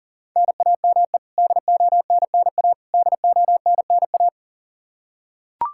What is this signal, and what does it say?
Morse code, heard as a keyed tone, about 40 wpm, 700 Hz. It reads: NAME DONNA DONNA
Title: Practice QSO - Exchange 1 - Beginning - 40wpm